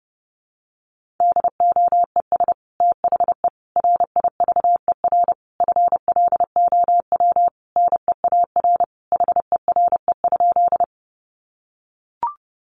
Morse code, heard as a keyed tone, about 30 wpm, 700 Hz. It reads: DOES THE RIVER FLOW NEAR HERE?
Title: Sentences from Top 1200 Words 30wpm